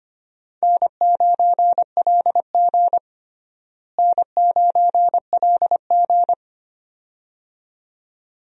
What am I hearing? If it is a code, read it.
Morse code, heard as a keyed tone, about 25 wpm, 700 Hz. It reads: N9LG N9LG
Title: USA Call Signs - 4 Characters - Repeated 2x - 25wpm